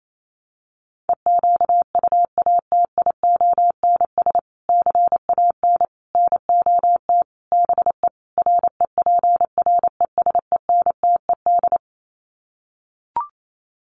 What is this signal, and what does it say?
Morse code, heard as a keyed tone, about 28 wpm, 700 Hz. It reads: EQUATIONS CAN NOT BE REPRESENTED
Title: Sets of 5 Words 28wpm